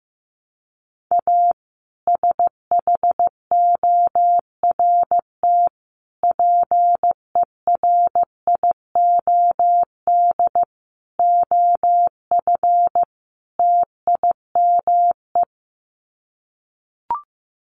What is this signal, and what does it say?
Morse code, heard as a keyed tone, about 15 wpm, 700 Hz. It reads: A SHORT PERIOD OF TIME
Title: Sets of 5 Words Encore 15wpm